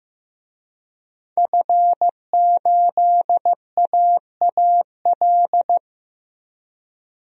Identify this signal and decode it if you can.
Morse code, heard as a keyed tone, about 15 wpm, 700 Hz. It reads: F8AAL
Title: DXCC-100 Call Signs - 5 Characters - 15wpm